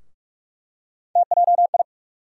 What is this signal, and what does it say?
Morse code, heard as a keyed tone, about 48 wpm, 700 Hz. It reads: TJI
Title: USA Call Sign Postfixes - 3 Letters 45wpm